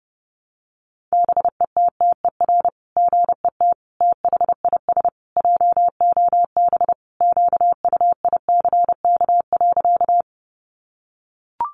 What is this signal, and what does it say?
Morse code, heard as a keyed tone, about 30 wpm, 700 Hz. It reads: BETTER GET THIS JOB QUICK.
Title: Sentences from Top 600 Words 30wpm